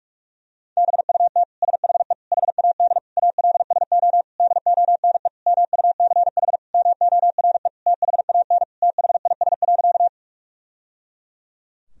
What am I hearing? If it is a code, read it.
Morse code, heard as a keyed tone, about 41 wpm, 700 Hz. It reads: BUT SHE HAD ALSO DONE MUCH MORE THAN THIS.
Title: Sentences from Top 1000 Words - Review Part 1 - 45wpm